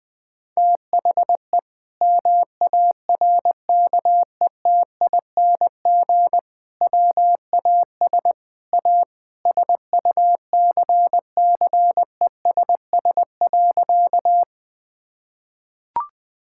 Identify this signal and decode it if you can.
Morse code, heard as a keyed tone, about 20 wpm, 700 Hz. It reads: THE MARKETING WAS A SUCCESS.
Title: Sentences from Top 1000 Words - Review Part 5 - 20wpm